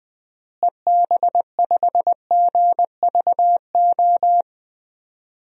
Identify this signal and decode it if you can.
Morse code, heard as a keyed tone, about 20 wpm, 700 Hz. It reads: EB5GVO
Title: DXCC-100 Call Signs - 6 Characters - Repeated 3x - 20wpm